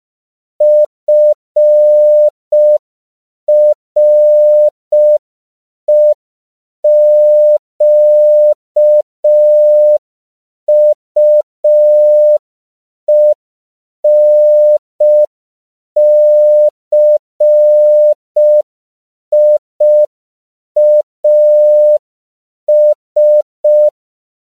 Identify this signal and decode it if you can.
Morse code, heard as a keyed tone, about 5 wpm, 600 Hz. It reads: FREQUENCIAS
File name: CW / 5PPM